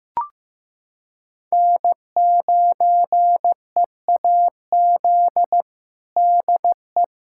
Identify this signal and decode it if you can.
Morse code, heard as a keyed tone, about 16 wpm, 700 Hz. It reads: N9EAZ DE
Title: USA Callsign DE Callsign 15wpm